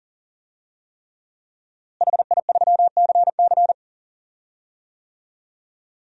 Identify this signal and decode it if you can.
Morse code, heard as a keyed tone, about 40 wpm, 700 Hz. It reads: HI3CC
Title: DXCC-100 Call Signs - 5 Characters - Repeated 3x - 40wpm